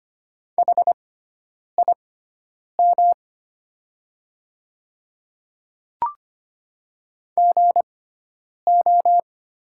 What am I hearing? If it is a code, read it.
Morse code, heard as a keyed tone, about 25 wpm, spaced as for 9 wpm, 700 Hz. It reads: HIM GO
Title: 6x Character Spacing - Top 100 Words - 25wpm